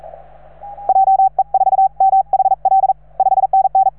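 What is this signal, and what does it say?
Morse code, heard as a keyed tone, about 40 wpm, 770 Hz. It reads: JE4MHL 5NN
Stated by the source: Morse from station 3V9A.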